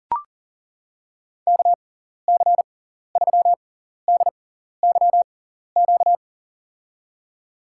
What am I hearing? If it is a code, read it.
Morse code, heard as a keyed tone, about 39 wpm, spaced as for 14 wpm, 690 Hz. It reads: KC3DYQ
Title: USA Call Signs - 6x Character Spacing - 40wpm